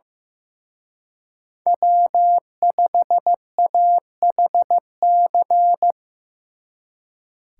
Morse code, heard as a keyed tone, about 18 wpm, 700 Hz. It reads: W5AHC
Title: Speed Racing - USA 1x3 Call Signs 15wpm